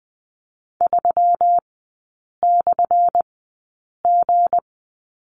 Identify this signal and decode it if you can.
Morse code, heard as a keyed tone, about 20 wpm, spaced as for 8 wpm, 700 Hz. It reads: 3/G
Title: 178 Lesson 29e - Slash - Three Characters 20wpm